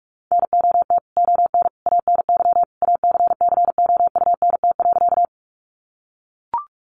Morse code, heard as a keyed tone, about 44 wpm, 700 Hz. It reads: NOT ON ANY ACCOUNT.
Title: Sentences from Top 900 Words 45wpm